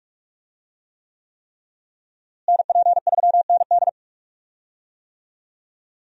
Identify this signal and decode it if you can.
Morse code, heard as a keyed tone, about 46 wpm, 700 Hz. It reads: NP3ND
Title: USA Call Signs - 5 Characters - Repeated 3x - 45wpm